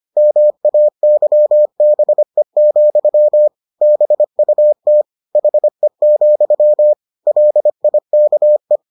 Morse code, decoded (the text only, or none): MAYBE, BUT HE, LIKE